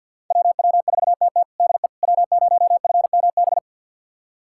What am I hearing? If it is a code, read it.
Morse code, heard as a keyed tone, about 46 wpm, 700 Hz. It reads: WW4TT DE W0FMB
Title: USA Callsign DE Callsign - 5 Characters - 50wpm